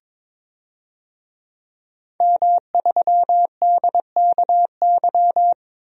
Morse code, heard as a keyed tone, about 21 wpm, 700 Hz. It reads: M3DKY
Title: DXCC-100 Call Signs - 5 Characters - 22wpm